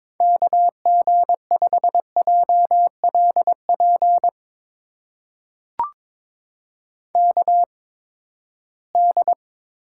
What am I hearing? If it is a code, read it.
Morse code, heard as a keyed tone, about 22 wpm, spaced as for 15 wpm, 700 Hz. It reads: KG5JLP KD
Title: USA Call Signs - 6 Characters - 8x Character Spacing - 22wpm